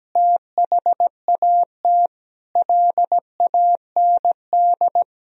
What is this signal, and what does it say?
Morse code, heard as a keyed tone, about 17 wpm, 700 Hz. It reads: THAT LAND
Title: Sentences from Top 1000 Words - Review Part 1 - 17wpm